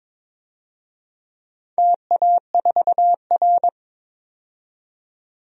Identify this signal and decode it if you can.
Morse code, heard as a keyed tone, about 22 wpm, 700 Hz. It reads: TA4R
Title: DXCC-100 Call Signs - 4 Characters - 22wpm